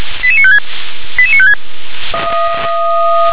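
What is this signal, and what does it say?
Five-tone selective call (ZVEI), heard as quick runs of five tones, tones of about 70 ms. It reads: 88956 88956
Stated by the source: Received on an SDR; ZVEI five-tone selective call.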